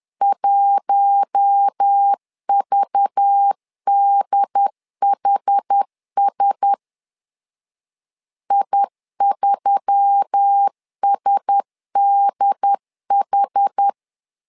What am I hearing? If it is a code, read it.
Morse code, heard as a keyed tone, about 10 wpm, 800 Hz. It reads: 1VDHS I3SDH
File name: LEZIONI / 5 LBVC / 5 LBVC 10wpm 2349